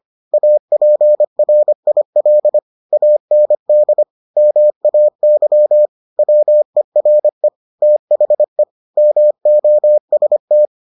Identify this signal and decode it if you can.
Morse code, heard as a keyed tone, about 25 wpm, 600 Hz. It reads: APRIL AND MAY WERE THE MOST